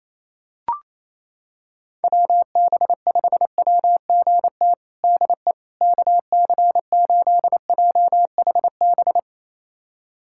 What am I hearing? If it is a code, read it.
Morse code, heard as a keyed tone, about 28 wpm, 700 Hz. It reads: WB5WGT DE KC8JHB
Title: USA Callsign DE Callsign 28wpm